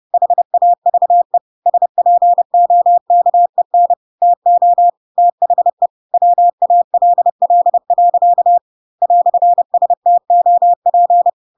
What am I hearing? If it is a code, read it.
Morse code, heard as a keyed tone, about 30 wpm, 700 Hz. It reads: HAVE SPOKEN TO THE WALL. "STOP
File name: Pinocchio / Pinocchio ch3 30 30 ik1hge